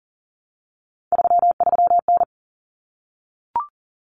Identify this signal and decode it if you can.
Morse code, heard as a keyed tone, about 40 wpm, 700 Hz. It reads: 33N
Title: Common Cut Numbers - 3-Digit 40wpm